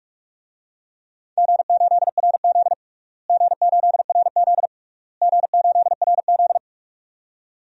Morse code, heard as a keyed tone, about 41 wpm, 700 Hz. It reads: G8RZ G8RZ G8RZ
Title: DXCC-100 Call Signs - 4 Characters - Repeated 3x - 45wpm